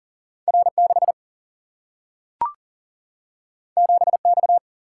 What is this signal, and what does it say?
Morse code, heard as a keyed tone, about 39 wpm, 700 Hz. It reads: R6 7X
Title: Single-Letter Single-Number 40wpm